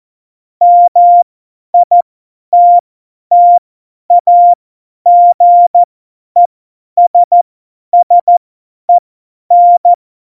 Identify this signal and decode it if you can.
Morse code, heard as a keyed tone, about 14 wpm, spaced as for 10 wpm, 700 Hz. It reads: MITTAGESSEN
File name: Morsekurs / Witze / Witz 25. Februar 2020, 50 F70